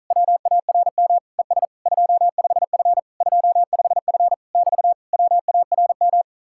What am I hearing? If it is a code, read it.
Morse code, heard as a keyed tone, about 48 wpm, 700 Hz. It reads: WARM ES 25F 25F <BT> WARM